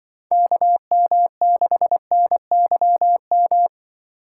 Morse code, heard as a keyed tone, about 24 wpm, 700 Hz. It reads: KM6NYM
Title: Speed Racing - USA 2x3 Call Signs 20wpm